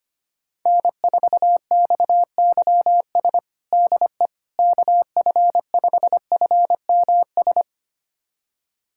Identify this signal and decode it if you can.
Morse code, heard as a keyed tone, about 25 wpm, 700 Hz. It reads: N4XYS DE KF5FMS
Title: USA Callsign DE Callsign 25wpm